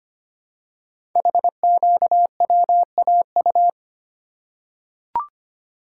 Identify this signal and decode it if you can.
Morse code, heard as a keyed tone, about 25 wpm, 700 Hz. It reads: HQWAU
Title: Code Groups - 5 Letters - 25wpm